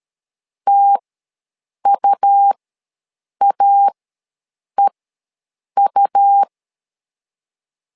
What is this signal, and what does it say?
Morse code, heard as a keyed tone, about 13 wpm, spaced as for 7 wpm, 800 Hz. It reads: TUAEU
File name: LEZIONI / 2 UHDN / 2 U ws 12wpm 1655